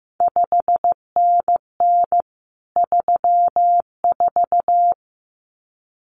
Morse code, heard as a keyed tone, about 15 wpm, 700 Hz. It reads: 5NN 34
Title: CQ WW DX Contest Exchange Encore - 15wpm